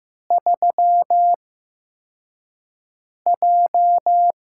Morse code, heard as a keyed tone, about 15 wpm, spaced as for 4 wpm, 700 Hz. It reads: 3J
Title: USA Call Signs - 4 Characters - 8x Character Spacing - 15wpm